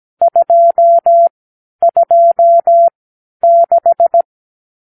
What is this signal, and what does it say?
Morse code, heard as a keyed tone, about 17 wpm, spaced as for 10 wpm, 680 Hz. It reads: 226
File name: k7qo / Lesson 50 (1-6)